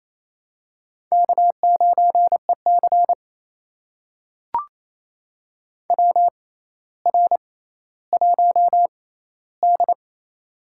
Morse code, heard as a keyed tone, about 28 wpm, spaced as for 14 wpm, 700 Hz. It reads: K9EC WR1D
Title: USA Call Signs - 4 Characters - 6x Character Spacing - 28wpm